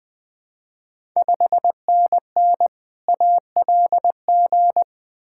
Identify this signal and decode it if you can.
Morse code, heard as a keyed tone, about 20 wpm, 700 Hz. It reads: 5NN ALG
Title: Ontario QSO Party Exchange 20wpm